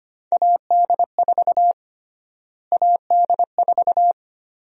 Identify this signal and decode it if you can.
Morse code, heard as a keyed tone, about 25 wpm, 700 Hz. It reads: AD4 AD4
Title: USA Call Sign Prefixes - Repeated 2x - 25wpm